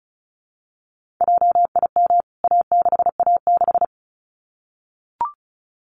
Morse code, heard as a keyed tone, about 34 wpm, 700 Hz. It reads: JIM A6A6
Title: CWT Exchange 35wpm